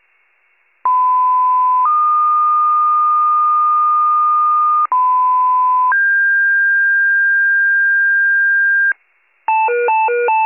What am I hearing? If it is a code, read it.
Two-tone page, heard as a one-second tone, then a three-second tone, about 1010 then 1220 Hz, about 1010 then 1600 Hz, with noise. It is heard from 0.8 to 8.9 s.